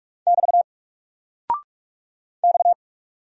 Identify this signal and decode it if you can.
Morse code, heard as a keyed tone, about 45 wpm, 700 Hz. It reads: <BT> X
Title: Single Character (Letter:Number:Punctuation:Prosign) Alt 45wpm